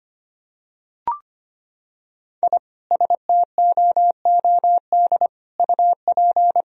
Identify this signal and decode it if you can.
Morse code, heard as a keyed tone, about 25 wpm, 700 Hz. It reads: I STOOD UP
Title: Sentences from Top 1000 Words - Review Part 2 - 25wpm